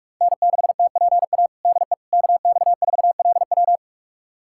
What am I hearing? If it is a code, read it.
Morse code, heard as a keyed tone, about 46 wpm, 700 Hz. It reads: N6TPA DE KX4LW
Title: USA Callsign DE Callsign - 5 Characters - 45wpm